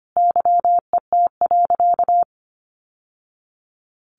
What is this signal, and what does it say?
Morse code, heard as a keyed tone, about 25 wpm, 700 Hz. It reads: YET.
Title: Sentences from Top 1100 Words 25wpm